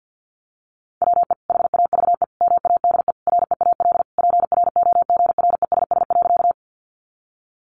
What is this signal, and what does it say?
Morse code, heard as a keyed tone, about 50 wpm, 700 Hz. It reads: WE HAVE MADE REAL PROGRESS.
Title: Sentences from Top 1400 Words 50wpm